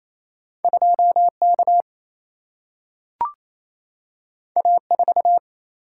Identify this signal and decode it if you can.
Morse code, heard as a keyed tone, about 28 wpm, 700 Hz. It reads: 2K A4
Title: Single-Letter Single-Number 28wpm